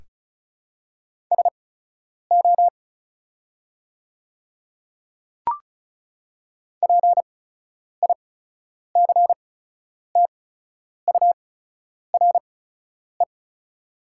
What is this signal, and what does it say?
Morse code, heard as a keyed tone, about 36 wpm, spaced as for 10 wpm, 700 Hz. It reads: SO PICTURE
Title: 8x Character Spacing - Top 500 Words - 35wpm